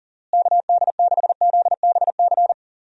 Morse code, heard as a keyed tone, about 40 wpm, 700 Hz. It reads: KD6ZBC
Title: USA Call Signs - 6 Characters - Repeated 3x - 40wpm